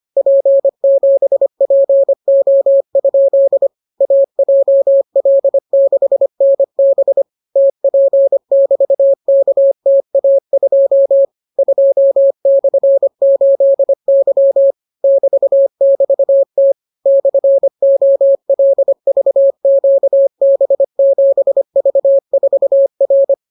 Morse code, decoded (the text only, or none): P7PO? AJL6NB TP<BT>KTA2 2/8Y <BT><BT>T /OLVQB7V4R